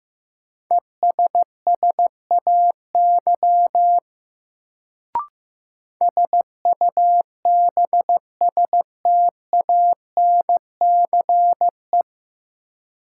Morse code, heard as a keyed tone, about 15 wpm, 700 Hz. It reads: ESSAY SUBSTANCE
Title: Sentences from Top 1500 Words - New Word Drill - 15wpm